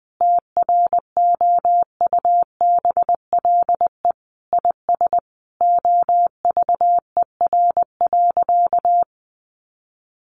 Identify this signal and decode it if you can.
Morse code, heard as a keyed tone, about 20 wpm, 700 Hz. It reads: TROUBLE IS OVER.
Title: Sentences from Top 1200 Words 20wpm